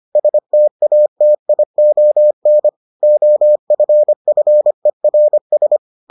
Morse code, heard as a keyed tone, about 25 wpm, 600 Hz. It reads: STATION OFFERS